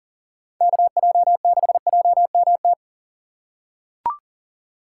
Morse code, heard as a keyed tone, about 38 wpm, 700 Hz. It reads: KJ6JMT
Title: USA Call Signs - 6 Characters - 4x Character Spacing - 40wpm